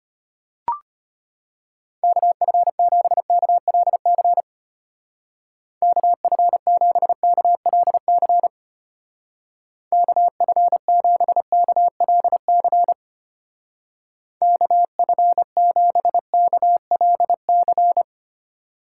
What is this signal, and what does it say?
Morse code, heard as a keyed tone, about 31 wpm, 700 Hz. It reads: KF7KLC KF7KLC KF7KLC KF7KLC